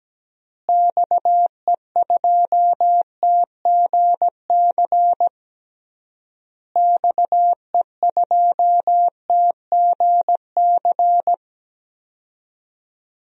Morse code, heard as a keyed tone, about 17 wpm, 700 Hz. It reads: XE2TGC XE2TGC
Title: DXCC-100 Call Signs - 6 Characters - Repeated 3x - 17wpm